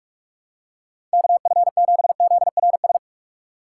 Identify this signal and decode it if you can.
Morse code, heard as a keyed tone, about 45 wpm, 700 Hz. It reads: KF7ZRS
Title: USA Call Signs - 6 Characters - 2x Character Spacing - 45wpm